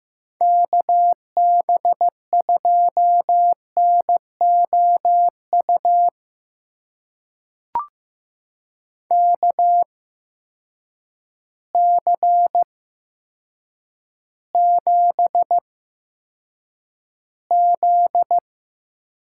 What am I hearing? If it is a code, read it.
Morse code, heard as a keyed tone, about 15 wpm, spaced as for 8 wpm, 700 Hz. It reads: KB2NOU KC7Z
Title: USA Call Signs - 8x Character Spacing - 15wpm